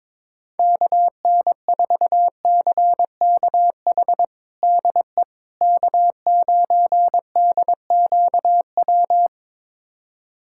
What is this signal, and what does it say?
Morse code, heard as a keyed tone, about 22 wpm, 700 Hz. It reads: KN4CKH DE K9DQW
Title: USA Callsign DE Callsign 22wpm